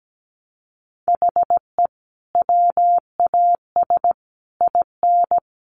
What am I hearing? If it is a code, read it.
Morse code, heard as a keyed tone, about 17 wpm, 700 Hz. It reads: HE WAS IN
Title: Sentences from Top 1300 Words 17wpm